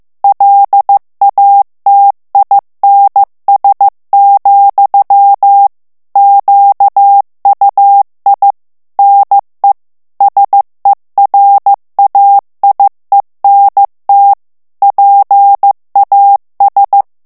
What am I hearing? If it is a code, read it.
Morse code, heard as a keyed tone, about 15 wpm, 800 Hz. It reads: LATINS, QUI NE SERAIENT PAS
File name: FRANCAIS TEXTE CLAIR 2 VITESSE 14 MOTS MINUTES